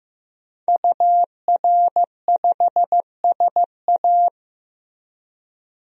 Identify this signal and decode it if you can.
Morse code, heard as a keyed tone, about 15 wpm, 700 Hz. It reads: UR5SA
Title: DXCC-100 Call Signs - 5 Characters - 15wpm